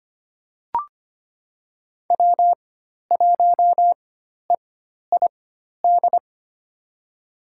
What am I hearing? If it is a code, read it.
Morse code, heard as a keyed tone, about 25 wpm, spaced as for 12 wpm, 700 Hz. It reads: W1EID
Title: USA Call Signs - 5 Characters - 4x Character Spacing - 25wpm